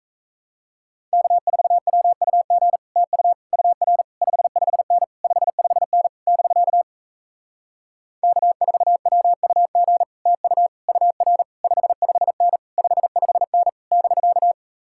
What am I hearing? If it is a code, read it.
Morse code, heard as a keyed tone, about 37 wpm, 700 Hz. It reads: K4WUG TU UR 55N 55N <BK> K4WUG TU UR 55N 55N <BK>